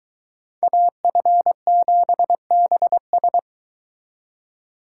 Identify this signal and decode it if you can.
Morse code, heard as a keyed tone, about 24 wpm, 700 Hz. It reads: AF7BS